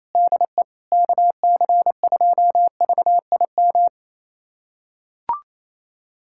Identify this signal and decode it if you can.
Morse code, heard as a keyed tone, about 28 wpm, 700 Hz. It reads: DE KC2VIM
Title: USA Callsign DE Callsign - 6 Characters - 28wpm